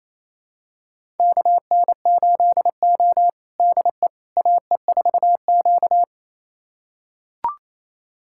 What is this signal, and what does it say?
Morse code, heard as a keyed tone, about 28 wpm, 700 Hz. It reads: KN8O DE AE4Q
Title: USA Callsign DE Callsign - 4 Characters - 28wpm